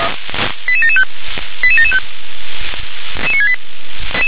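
Five-tone selective call (ZVEI), heard as quick runs of five tones, tones of about 70 ms. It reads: 88775 88775
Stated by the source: Five-tone ZVEI selective call 88965, received on SDR.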